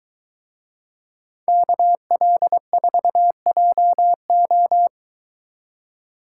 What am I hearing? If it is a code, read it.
Morse code, heard as a keyed tone, about 22 wpm, 700 Hz. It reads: KL4JO